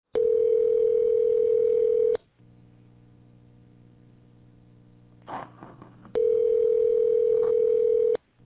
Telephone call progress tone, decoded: ringback tone

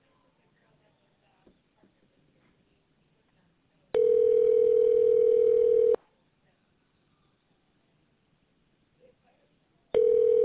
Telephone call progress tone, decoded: ringback tone

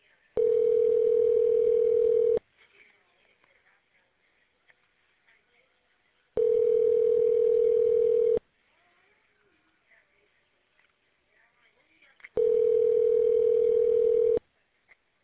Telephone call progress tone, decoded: ringback tone

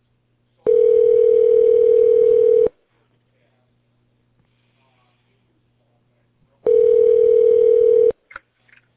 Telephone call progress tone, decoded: ringback tone

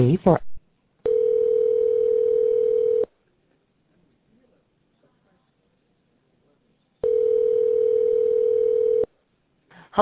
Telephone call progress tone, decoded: ringback tone

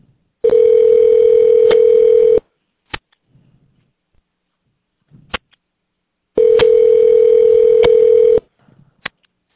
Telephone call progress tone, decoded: ringback tone

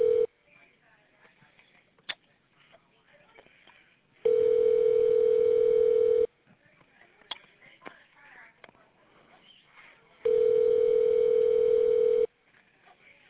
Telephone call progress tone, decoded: ringback tone

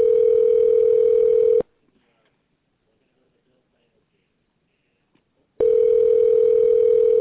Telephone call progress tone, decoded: ringback tone

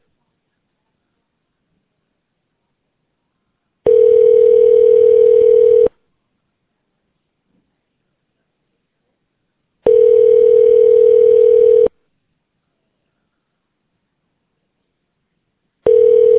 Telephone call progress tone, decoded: ringback tone

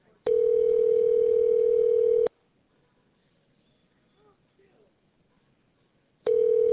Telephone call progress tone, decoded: ringback tone

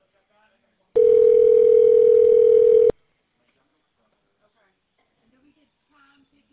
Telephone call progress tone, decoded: ringback tone

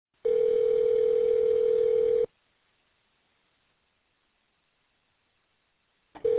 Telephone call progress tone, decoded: ringback tone

